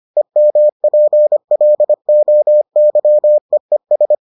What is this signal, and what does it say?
Morse code, heard as a keyed tone, about 25 wpm, 600 Hz. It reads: EMPLOYEES